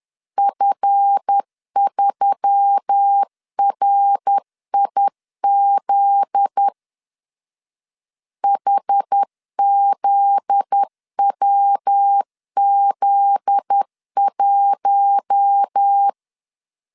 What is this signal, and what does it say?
Morse code, heard as a keyed tone, about 10 wpm, 800 Hz. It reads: F3RIZ HZWZ1
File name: LEZIONI / 7 GF9Z / 7 GF9Z 10wpm 2611